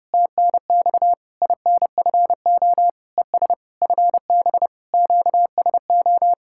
Morse code, heard as a keyed tone, about 33 wpm, 700 Hz. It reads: TNX INFO ES FB QSO